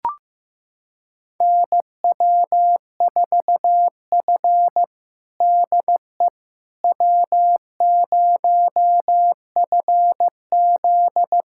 Morse code, heard as a keyed tone, about 15 wpm, 700 Hz. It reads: NW4F DE W0FZ
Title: USA Callsign DE Callsign - 4 Characters - 15wpm